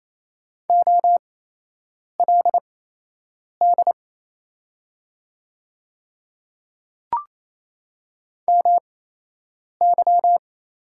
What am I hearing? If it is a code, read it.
Morse code, heard as a keyed tone, about 28 wpm, spaced as for 8 wpm, 700 Hz. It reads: OLD MY
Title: 8x Character Spacing - Top 200 Words - 28wpm